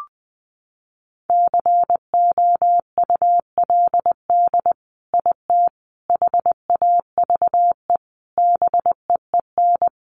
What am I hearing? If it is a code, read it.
Morse code, heard as a keyed tone, about 20 wpm, 700 Hz. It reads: COULD IT HAVE BEEN
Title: Sentences from Top 1200 Words 20wpm